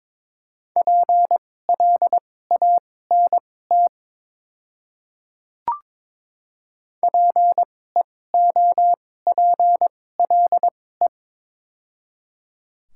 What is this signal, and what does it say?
Morse code, heard as a keyed tone, about 22 wpm, spaced as for 16 wpm, 700 Hz. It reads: PLANT PEOPLE
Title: 2x Character Spacing - Top 300 Words - 22wpm